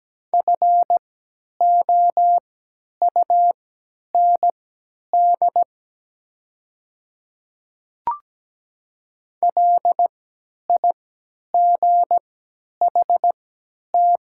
Morse code, heard as a keyed tone, about 17 wpm, spaced as for 10 wpm, 700 Hz. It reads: FOUND LIGHT
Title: 3x Character Spacing - Top 500 Words - 17wpm